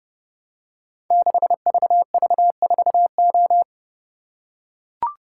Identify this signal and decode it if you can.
Morse code, heard as a keyed tone, about 30 wpm, 700 Hz. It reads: 6VV4O
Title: Code Groups - 5 Letter:Numbers - 30wpm